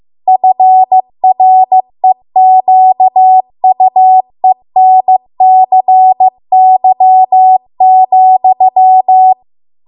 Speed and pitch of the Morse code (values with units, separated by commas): 15 wpm, 750 Hz